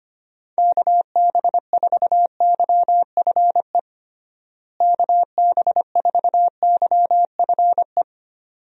Morse code, heard as a keyed tone, about 25 wpm, 700 Hz. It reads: KB4YFE KB4YFE
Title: USA Call Signs - 6 Characters - Repeated 2x - 25wpm